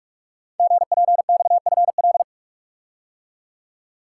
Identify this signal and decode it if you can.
Morse code, heard as a keyed tone, about 44 wpm, 700 Hz. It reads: GPXFL
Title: Code Groups - 5 Letters - 45wpm